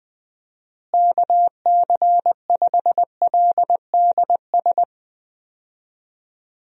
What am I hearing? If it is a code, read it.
Morse code, heard as a keyed tone, about 20 wpm, 700 Hz. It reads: KC5LDS
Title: USA Call Signs - 6 Characters - Repeated 3x - 20wpm